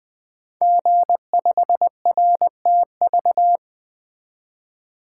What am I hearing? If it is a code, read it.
Morse code, heard as a keyed tone, about 20 wpm, 700 Hz. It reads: G5RTV
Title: Code Groups - 5 Letter:Numbers - 20wpm